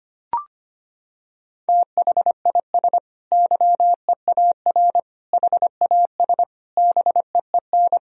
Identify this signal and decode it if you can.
Morse code, heard as a keyed tone, about 25 wpm, 700 Hz. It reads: THIS YEAR HAS BEEN
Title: Sentences from Top 1000 Words - Review Part 3 - 25wpm